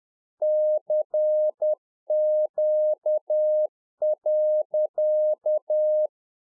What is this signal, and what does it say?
Morse code, heard as a keyed tone, about 10 wpm, 610 Hz. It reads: CQ.
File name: UNFOGETTABLE QSO 10WPM